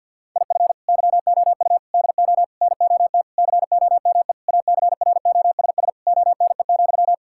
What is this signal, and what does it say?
Morse code, heard as a keyed tone, about 52 wpm, 700 Hz. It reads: IF YOU DO NOT COME ACROSS ONE,
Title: Sentences from Top 1000 Words - Review Part 5 - 50wpm